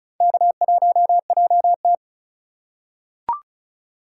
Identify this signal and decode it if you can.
Morse code, heard as a keyed tone, about 36 wpm, 700 Hz. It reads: K1JT
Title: USA Call Signs - 4 Characters - 4x Character Spacing - 35wpm